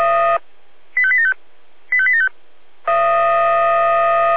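Five-tone selective call (ZVEI), heard as quick runs of five tones, tones of about 70 ms. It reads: 85875 85875
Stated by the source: ZVEI five-tone selective call 85840 received by SDR.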